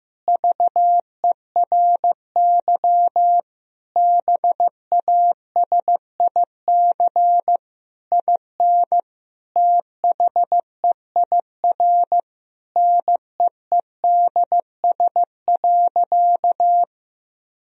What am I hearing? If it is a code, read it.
Morse code, heard as a keyed tone, about 16 wpm, 700 Hz. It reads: VERY BASIC IN THEIR NEEDS.
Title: Sentences from Top 1000 Words - Review Part 3 - 15wpm